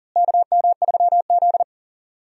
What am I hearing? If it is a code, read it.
Morse code, heard as a keyed tone, about 40 wpm, 700 Hz. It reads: KM3Z
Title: USA Call Signs - 4 Characters - Repeated 2x - 40wpm